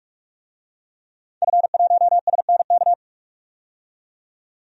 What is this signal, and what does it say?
Morse code, heard as a keyed tone, about 46 wpm, 700 Hz. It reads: F1SNK
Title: DXCC-100 Call Signs - 5 Characters - 45wpm